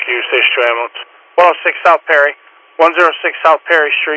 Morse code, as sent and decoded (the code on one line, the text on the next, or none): none